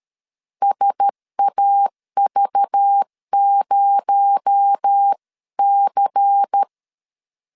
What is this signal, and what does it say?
Morse code, heard as a keyed tone, about 13 wpm, 800 Hz. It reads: SAV0C
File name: CW / DET / 1 6 1500C S C 12wpm 3533